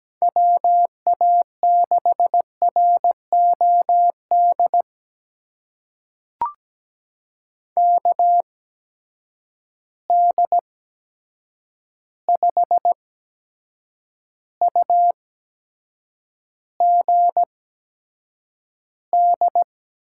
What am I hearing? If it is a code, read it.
Morse code, heard as a keyed tone, about 17 wpm, spaced as for 7 wpm, 700 Hz. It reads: WA6ROD KD5UGD
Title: USA Call Signs - 6 Characters - 8x Character Spacing - 17wpm